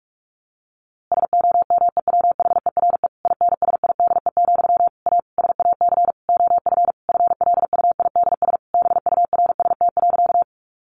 Morse code, heard as a keyed tone, about 44 wpm, 700 Hz. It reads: SOMEWHERE INSIDE, A SAC OF FLUIDS BURST.
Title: A Sound Of Thunder 45wpm